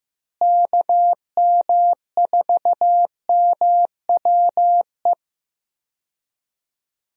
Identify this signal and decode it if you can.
Morse code, heard as a keyed tone, about 16 wpm, 700 Hz. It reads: KM4MWE
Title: USA Call Signs - 6 Characters - Repeated 3x - 15wpm